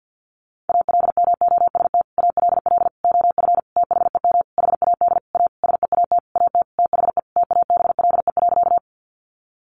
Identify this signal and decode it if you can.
Morse code, heard as a keyed tone, about 50 wpm, 700 Hz. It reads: ALMOST ALL OF THEM HAD A SEAT AT THE TABLE.
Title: Sentences from Top 1500 Words 50wpm